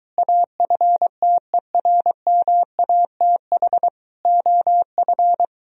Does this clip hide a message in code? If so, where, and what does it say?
Yes: Morse code from 0.2 to 5.4 s, heard as a keyed tone, about 23 wpm, 700 Hz. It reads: AFTERMATH OF